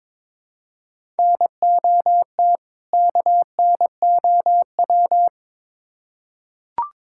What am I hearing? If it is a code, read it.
Morse code, heard as a keyed tone, about 22 wpm, 700 Hz. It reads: NOT KNOW
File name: Build A Phrase - 2-Word Phrases - 22wpm